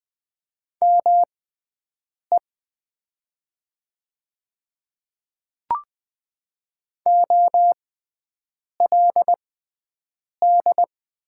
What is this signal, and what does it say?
Morse code, heard as a keyed tone, about 20 wpm, spaced as for 7 wpm, 700 Hz. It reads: ME OLD
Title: 6x Character Spacing - Top 200 Words - 20wpm